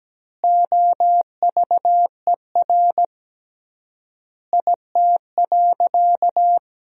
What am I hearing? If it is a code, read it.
Morse code, heard as a keyed tone, about 17 wpm, 700 Hz. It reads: OVER IT.
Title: 3x Word-Spacing - Sentences from Top 100 Words - 17wpm